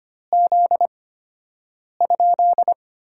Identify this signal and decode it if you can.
Morse code, heard as a keyed tone, about 25 wpm, spaced as for 7 wpm, 700 Hz. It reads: Z?
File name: Top 90 QSO Elements - 8x Character Spacing - 25wpm